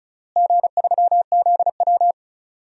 Morse code, heard as a keyed tone, about 36 wpm, 700 Hz. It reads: G3ZW
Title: DXCC-100 Call Signs - 4 Characters - Repeated 3x - 35wpm